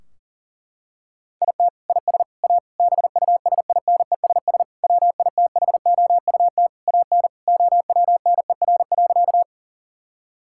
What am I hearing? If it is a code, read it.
Morse code, heard as a keyed tone, about 37 wpm, 700 Hz. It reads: IT IS A BUSINESS WITHOUT AN OWNER.
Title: Sentences from Top 1600 Words 40wpm